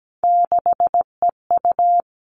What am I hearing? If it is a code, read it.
Morse code, heard as a keyed tone, about 17 wpm, 700 Hz. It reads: 6EU
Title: All Cut Numbers - 3-Digit 17wpm